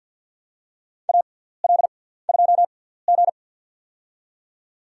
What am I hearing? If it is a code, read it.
Morse code, heard as a keyed tone, about 51 wpm, spaced as for 17 wpm, 700 Hz. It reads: AL2G
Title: USA Call Signs - 4 Characters - 6x Character Spacing - 50wpm